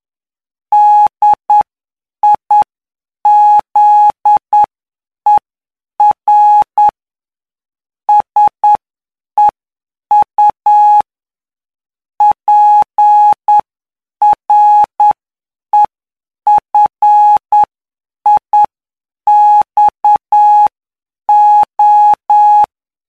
Morse code, heard as a keyed tone, about 9 wpm, spaced as for 7 wpm, 840 Hz. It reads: DIZER SEU PREFIXO
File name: cw / py2lcd / 19-PY2LCD-Li��o 28